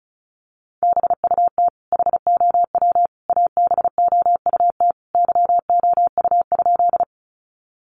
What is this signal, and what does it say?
Morse code, heard as a keyed tone, about 34 wpm, 700 Hz. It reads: BUT HOW ABOUT YOU?
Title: Sentences from Top 100 Words 35wpm